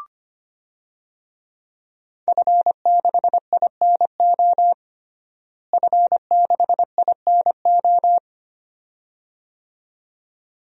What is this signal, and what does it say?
Morse code, heard as a keyed tone, about 25 wpm, 700 Hz. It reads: F6INO F6INO
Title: DXCC-100 Call Signs - 5 Characters - Repeated 2x - 25wpm